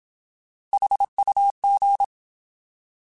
Morse code, heard as a keyed tone, about 27 wpm, 800 Hz. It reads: HUG
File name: Alphabetical / Alphabetical mp3s / Hs @25